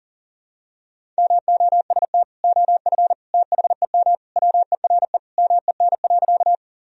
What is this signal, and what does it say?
Morse code, heard as a keyed tone, about 40 wpm, 700 Hz. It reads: MOST OF THEM WERE MEN.
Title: Sentences from Top 1000 Words - Review Part 1 - 40wpm